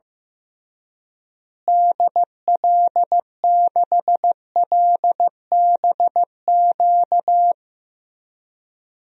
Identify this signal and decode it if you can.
Morse code, heard as a keyed tone, about 15 wpm, 700 Hz. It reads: DL6LBQ
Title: DXCC-100 Call Signs - 6 Characters - Repeated 3x - 15wpm